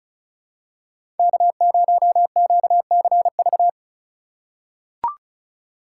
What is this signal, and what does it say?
Morse code, heard as a keyed tone, about 34 wpm, 700 Hz. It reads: K0QCV
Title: USA Call Signs - 3x Character Spacing - 35wpm